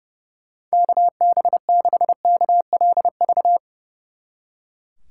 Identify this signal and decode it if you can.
Morse code, heard as a keyed tone, about 37 wpm, 700 Hz. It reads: KB6KLV
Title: Speed Racing - USA Call Signs 30wpm